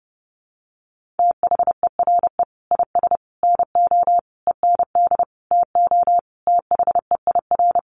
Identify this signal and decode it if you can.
Morse code, heard as a keyed tone, about 30 wpm, 700 Hz. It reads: THERE IS NO END TO THEIR
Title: Sentences from Top 1100 Words 30wpm